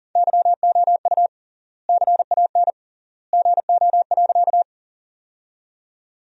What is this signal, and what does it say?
Morse code, heard as a keyed tone, about 40 wpm, 700 Hz. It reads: YOU CAN GO.
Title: 3x Word-Spacing - Sentences from Top 100 Words - 40wpm